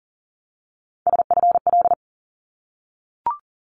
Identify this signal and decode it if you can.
Morse code, heard as a keyed tone, about 38 wpm, 700 Hz. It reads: SFL
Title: ARRL Sections 40wpm